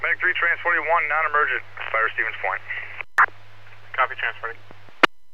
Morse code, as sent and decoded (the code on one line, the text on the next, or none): none